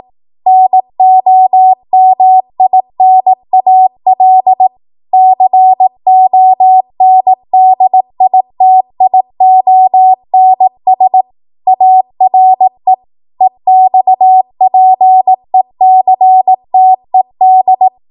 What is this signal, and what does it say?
Morse code, heard as a keyed tone, about 18 wpm, 750 Hz. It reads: NOMINAL CONDITIONS ARE EXPECTED
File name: Morse / 18WPM